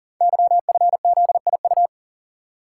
Morse code, heard as a keyed tone, about 38 wpm, 700 Hz. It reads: YFZIU
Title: Code Groups - 5 Letters - 40wpm